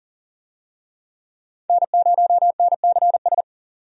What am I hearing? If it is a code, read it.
Morse code, heard as a keyed tone, about 40 wpm, 700 Hz. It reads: N0NCS
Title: USA Call Signs - 5 Characters - 40wpm